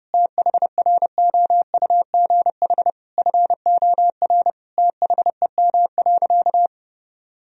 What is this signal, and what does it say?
Morse code, heard as a keyed tone, about 29 wpm, 700 Hz. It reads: THROUGH FOR THEM.
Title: Sentences from Top 1000 Words - Review Part 1 - 30wpm